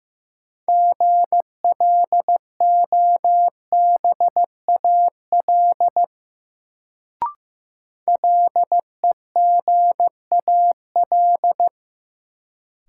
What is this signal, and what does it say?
Morse code, heard as a keyed tone, about 15 wpm, 700 Hz. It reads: GLOBAL LEGAL
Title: Sentences from Top 1200 Words - New Word Drill - 15wpm